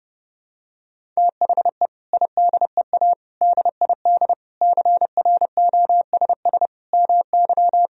Morse code, heard as a keyed tone, about 30 wpm, 700 Hz. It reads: THE IDEA DID CROSS MY
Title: Sentences from Top 1000 Words - Review Part 1 - 30wpm